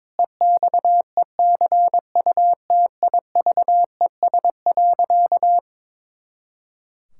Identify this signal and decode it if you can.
Morse code, heard as a keyed tone, about 22 wpm, 700 Hz. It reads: EXECUTIVES.
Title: Pangram Challenge v2 - 22wpm